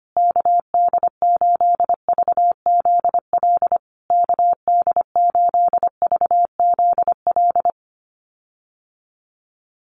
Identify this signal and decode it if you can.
Morse code, heard as a keyed tone, about 25 wpm, 700 Hz. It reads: KD8VZL KD8VZL
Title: USA Call Signs Repeated 3x 25wpm